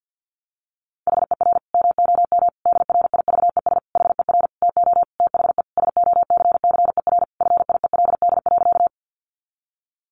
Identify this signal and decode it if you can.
Morse code, heard as a keyed tone, about 50 wpm, 700 Hz. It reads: HER MOM DRIVES HER TO THE SOCCER FIELD.
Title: Easy Phrases 50wpm